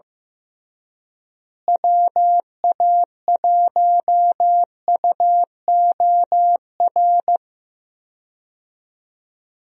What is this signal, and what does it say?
Morse code, heard as a keyed tone, about 15 wpm, 700 Hz. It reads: WA1UOR
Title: USA Call Signs - 6 Characters - Repeated 3x - 15wpm